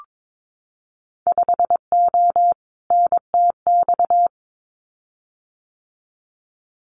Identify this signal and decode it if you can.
Morse code, heard as a keyed tone, about 22 wpm, 700 Hz. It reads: 5O NTX
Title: Winter Field Day Exchanges Encore 22wpm